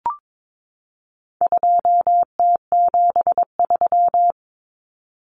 Morse code, heard as a keyed tone, about 22 wpm, 700 Hz. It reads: 2T73
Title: Common Cut Numbers - 4-Digit 22wpm